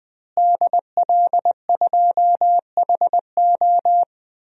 Morse code, heard as a keyed tone, about 20 wpm, 700 Hz. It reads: DL2HO
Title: DXCC-100 Call Signs - 5 Characters - Repeated 2x - 20wpm